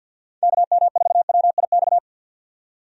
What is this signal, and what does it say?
Morse code, heard as a keyed tone, about 52 wpm, 700 Hz. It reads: KM4WIX
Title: USA Call Signs - 6 Characters - 50wpm